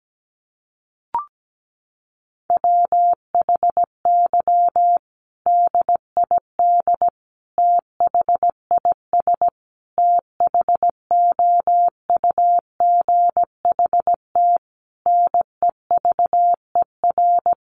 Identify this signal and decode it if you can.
Morse code, heard as a keyed tone, about 17 wpm, 700 Hz. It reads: WHY DID THIS THOUGHT NEVER
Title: Sentences from Top 900 Words 17wpm